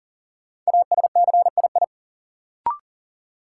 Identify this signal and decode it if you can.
Morse code, heard as a keyed tone, about 40 wpm, 700 Hz. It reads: ASCII
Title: Sets of 5 Words - New Word Drill - 40wpm